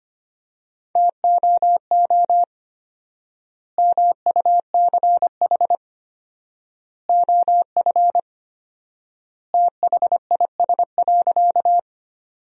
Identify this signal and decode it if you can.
Morse code, heard as a keyed tone, about 25 wpm, 700 Hz. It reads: TOO MUCH OF THIS.
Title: 4x Word-Spacing - Sentences from Top 200 Words - 25wpm